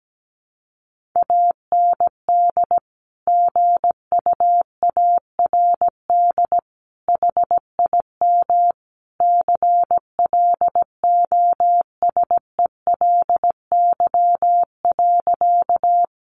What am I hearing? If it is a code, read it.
Morse code, heard as a keyed tone, about 17 wpm, 700 Hz. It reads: AND GUARD HIM CLOSELY.
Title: Sentences from Top 1400 Words 17wpm